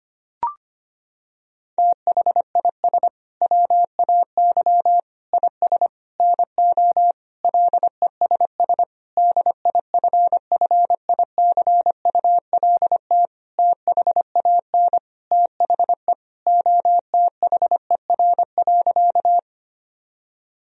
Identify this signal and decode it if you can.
Morse code, heard as a keyed tone, about 25 wpm, 700 Hz. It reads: THIS WAY IS NO LESS DIFFICULT THAN THE OTHER.
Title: Sentences from Top 1000 Words - Review Part 2 - 25wpm